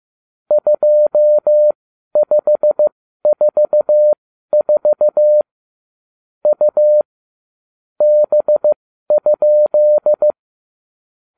Morse code, heard as a keyed tone, about 15 wpm, spaced as for 12 wpm, 600 Hz. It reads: 2544 U B?
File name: cw / ubacw34 3